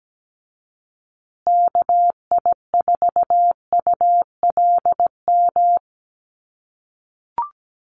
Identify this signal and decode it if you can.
Morse code, heard as a keyed tone, about 17 wpm, 700 Hz. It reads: KI4ULM
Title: USA Call Signs Repeated 3x 17wpm